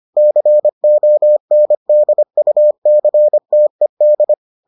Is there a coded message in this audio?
Yes: Morse code from 0.2 to 4.3 s, heard as a keyed tone, about 25 wpm, 600 Hz. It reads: CONDUCTED